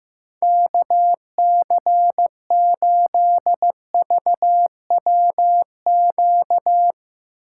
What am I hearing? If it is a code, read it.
Morse code, heard as a keyed tone, about 16 wpm, 700 Hz. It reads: KC8VWQ
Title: Speed Racing - USA 2x3 Call Signs 15wpm